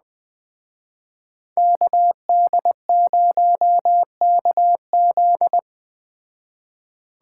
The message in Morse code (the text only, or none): KD0KZ